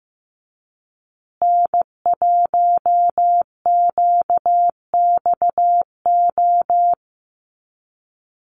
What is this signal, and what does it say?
Morse code, heard as a keyed tone, about 15 wpm, 700 Hz. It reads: N1QXO
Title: USA Call Signs Repeated 2x 15wpm